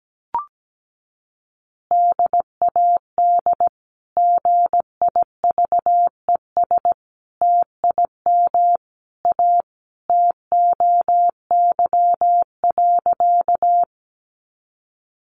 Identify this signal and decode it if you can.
Morse code, heard as a keyed tone, about 17 wpm, 700 Hz. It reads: DAD GIVES TIM A TOY.
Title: Easy Phrases 17wpm